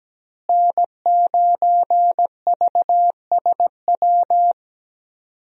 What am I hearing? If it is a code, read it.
Morse code, heard as a keyed tone, about 17 wpm, 700 Hz. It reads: N9VSW
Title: USA Call Signs - 5 Characters - Repeated 2x - 17wpm